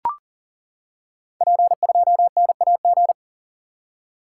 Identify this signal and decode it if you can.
Morse code, heard as a keyed tone, about 40 wpm, 700 Hz. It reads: P2NAG
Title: Code Groups - 5 Letter:Numbers - 40wpm